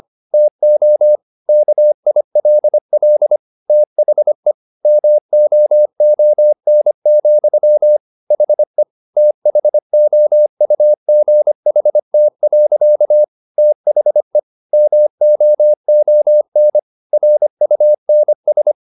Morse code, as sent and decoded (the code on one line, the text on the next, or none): - --- / -.- .. .-.. .-.. / - .... / -- --- --- -. --..-- / .... / - .... --- ..- --. .... - .-.-.- / - .... / -- --- --- -. / .-. ..- -. ...
TO KILL THE MOON, HE THOUGHT. THE MOON RUNS